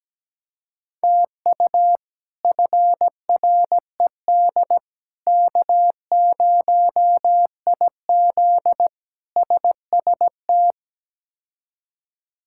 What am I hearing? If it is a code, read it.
Morse code, heard as a keyed tone, about 17 wpm, 700 Hz. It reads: TU FRED K0IZ SST
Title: K1USN SST QSO Practice - 17wpm